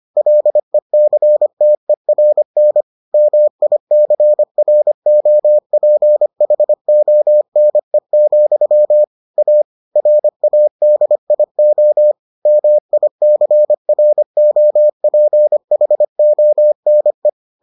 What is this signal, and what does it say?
Morse code, heard as a keyed tone, about 25 wpm, 600 Hz. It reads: LECTERN MICROPHONE, A RADIO MICROPHONE